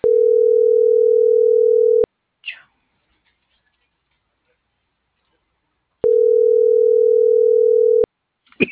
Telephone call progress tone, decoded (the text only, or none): ringback tone